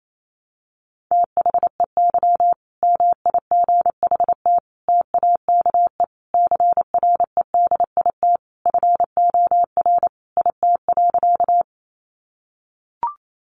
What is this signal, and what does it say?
Morse code, heard as a keyed tone, about 28 wpm, 700 Hz. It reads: THEY MIGHT TAKE CREDIT FOR IT.
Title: Sentences from Top 1000 Words 28wpm